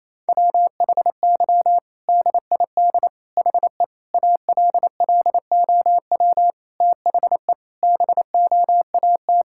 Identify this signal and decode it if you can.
Morse code, heard as a keyed tone, about 28 wpm, 700 Hz. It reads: WHY DID HE ALLOW THE BOAT
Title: Sentences from Top 1000 Words - Review Part 5 - 28wpm